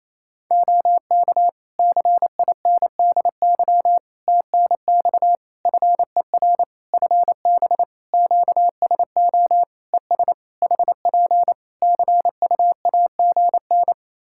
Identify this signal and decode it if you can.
Morse code, heard as a keyed tone, about 28 wpm, 700 Hz. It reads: OK CINDY TNX FER FB QSO ES HP CUAGN
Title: K1USN SST QSO Practice 28wpm